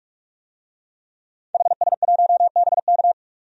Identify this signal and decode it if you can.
Morse code, heard as a keyed tone, about 46 wpm, 700 Hz. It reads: HS1BK
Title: DXCC-100 Call Signs - 5 Characters - 45wpm